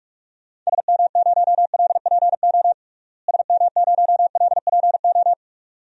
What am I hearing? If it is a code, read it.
Morse code, heard as a keyed tone, about 41 wpm, 700 Hz. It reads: SM0LPO SM0LPO
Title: DXCC-100 Call Signs - 6 Characters - Repeated 2x - 45wpm